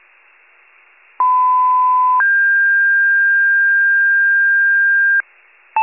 Two-tone page, heard as a one-second tone, then a three-second tone, about 1010 then 1600 Hz. It is heard from 1.2 to 5.2 s.